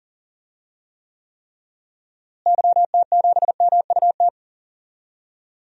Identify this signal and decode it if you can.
Morse code, heard as a keyed tone, about 40 wpm, 700 Hz. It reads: YT7MUT
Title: DXCC-100 Call Signs - 6 Characters - Repeated 3x - 40wpm